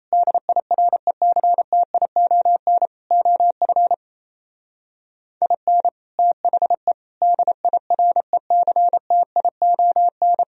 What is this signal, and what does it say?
Morse code, heard as a keyed tone, about 35 wpm, 700 Hz. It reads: DIRECTION OF IN THE DIRECTION
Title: Speed-Racing - Sets of 4 Words 28wpm